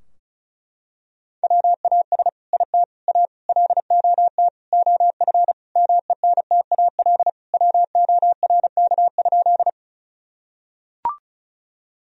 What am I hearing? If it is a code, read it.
Morse code, heard as a keyed tone, about 36 wpm, 700 Hz. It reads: WAS IT A LOT OF MENTAL WORK?
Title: Sentences from Top 1000 Words - Review Part 4 - 35wpm